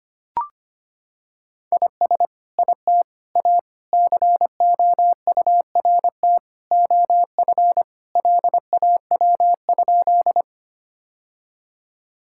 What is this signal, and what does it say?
Morse code, heard as a keyed tone, about 25 wpm, 700 Hz. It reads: IS IT A COURT OF LAW?
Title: Sentences from Top 1600 Words 25wpm